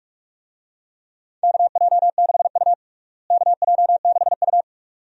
Morse code, heard as a keyed tone, about 41 wpm, 700 Hz. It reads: KJ6U KJ6U
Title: USA Call Signs - 4 Characters - Repeated 2x - 45wpm